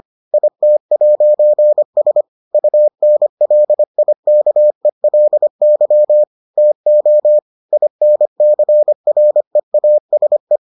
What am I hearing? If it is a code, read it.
Morse code, heard as a keyed tone, about 25 wpm, 600 Hz. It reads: IT'S UNLIKELY TO INCREASE